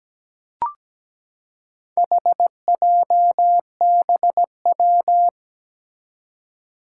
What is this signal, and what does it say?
Morse code, heard as a keyed tone, about 17 wpm, 700 Hz. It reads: HJBW
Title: Four Challenging Letter-Number Combinations 17wpm